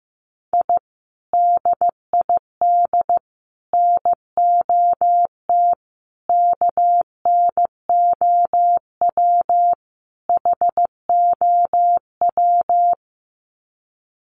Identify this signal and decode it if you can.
Morse code, heard as a keyed tone, about 15 wpm, 700 Hz. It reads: I DID NOT KNOW HOW
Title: Sets of 5 Words Encore 15wpm